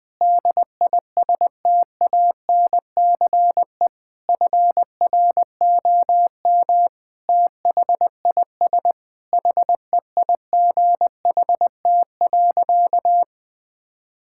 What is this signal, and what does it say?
Morse code, heard as a keyed tone, about 20 wpm, 700 Hz. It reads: DISTANCE FROM THIS HEIGHT.
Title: Sentences from Top 1600 Words 20wpm